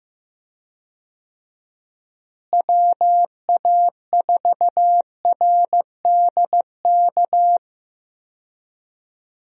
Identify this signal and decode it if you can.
Morse code, heard as a keyed tone, about 15 wpm, 700 Hz. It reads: WA4RDK
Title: USA Call Signs - 6 Characters - Repeated 3x - 15wpm